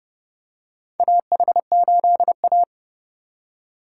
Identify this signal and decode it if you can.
Morse code, heard as a keyed tone, about 30 wpm, 700 Hz. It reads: AH8A
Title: USA Call Signs - 3x Character Spacing - 30wpm